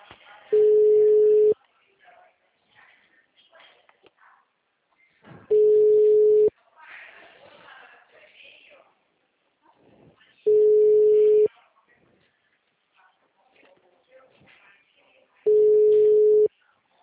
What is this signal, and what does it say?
Ringback tone (Europe, 425 Hz), with speech.
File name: out- -2005- - - .318